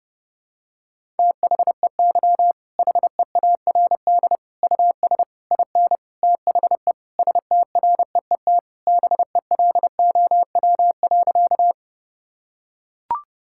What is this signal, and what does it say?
Morse code, heard as a keyed tone, about 30 wpm, 700 Hz. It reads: THEY HEARD US IN THE STREET BELOW.
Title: Sentences from Top 1000 Words - Review Part 5 - 30wpm